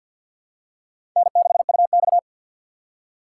Morse code, heard as a keyed tone, about 52 wpm, 700 Hz. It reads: N6UX
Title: USA Call Signs - 4 Characters - 6x Character Spacing - 50wpm